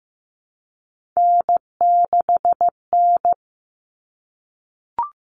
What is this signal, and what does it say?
Morse code, heard as a keyed tone, about 15 wpm, 700 Hz. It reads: N6N
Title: All Cut Numbers - 3-Digit 15wpm